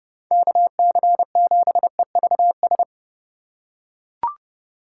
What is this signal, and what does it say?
Morse code, heard as a keyed tone, about 30 wpm, 700 Hz. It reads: KC7EVS
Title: USA Call Signs - 8x Character Spacing - 30wpm